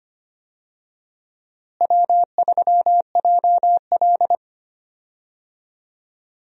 Morse code, heard as a keyed tone, about 25 wpm, 700 Hz. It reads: W3JL
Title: USA Call Signs - 4 Characters - Repeated 3x - 25wpm